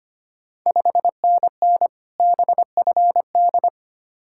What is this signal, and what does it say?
Morse code, heard as a keyed tone, about 25 wpm, 690 Hz. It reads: 5NN BFD
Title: Ontario QSO Party Exchange 25wpm